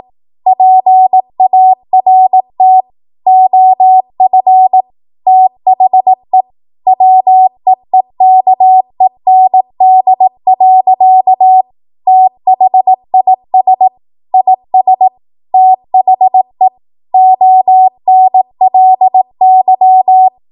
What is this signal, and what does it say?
Morse code, heard as a keyed tone, about 18 wpm, 750 Hz. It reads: PART OF THE WEEKEND. THIS IS THE ONLY
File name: Archive / 18 WPM / 18WPM